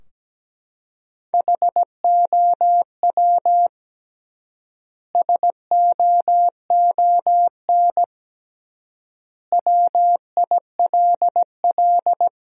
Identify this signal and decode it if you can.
Morse code, heard as a keyed tone, about 17 wpm, 700 Hz. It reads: HOW SOON WILL
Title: 3x Word-Spacing - Sentences from Top 400 Words - 17wpm